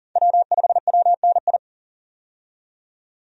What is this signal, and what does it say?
Morse code, heard as a keyed tone, about 37 wpm, 700 Hz. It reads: W5WNI
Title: USA Call Signs - 5 Characters - 40wpm